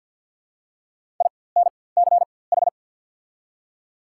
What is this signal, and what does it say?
Morse code, heard as a keyed tone, about 53 wpm, spaced as for 24 wpm, 690 Hz. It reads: INCH
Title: 4x Character Spacing - Top 500 Words - 50wpm